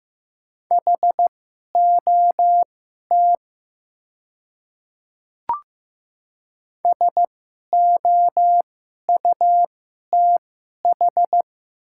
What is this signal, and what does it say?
Morse code, heard as a keyed tone, about 15 wpm, spaced as for 11 wpm, 700 Hz. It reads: HOT SOUTH
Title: 2x Character Spacing - Top 500 Words - 15wpm